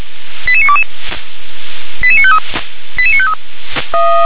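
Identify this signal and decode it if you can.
Five-tone selective call (ZVEI), heard as quick runs of five tones, tones of about 70 ms. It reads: 88922 88953 88953
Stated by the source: ZVEI five-tone selective call received by SDR.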